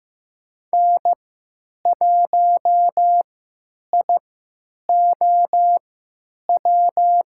Morse code, heard as a keyed tone, about 15 wpm, spaced as for 9 wpm, 700 Hz. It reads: N1IOW
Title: USA Call Signs - 5 Characters - 3x Character Spacing - 15wpm